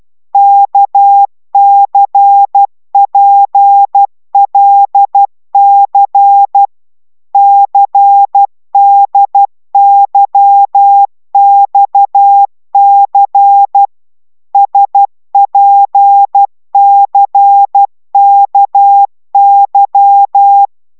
Morse code, heard as a keyed tone, about 12 wpm, 800 Hz. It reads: KCPLC CDYXC SPCKY
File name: taviro tanulas / cw 69 19 10